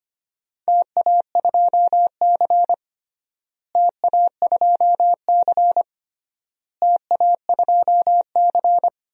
Morse code, heard as a keyed tone, about 25 wpm, 700 Hz. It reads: TA2C TA2C TA2C
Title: DXCC-100 Call Signs - 4 Characters - Repeated 3x - 25wpm